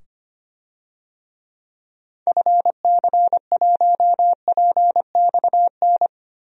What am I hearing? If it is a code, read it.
Morse code, heard as a keyed tone, about 25 wpm, 690 Hz. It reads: FC1PXN
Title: DXCC-100 Call Signs - 6 Characters - 25wpm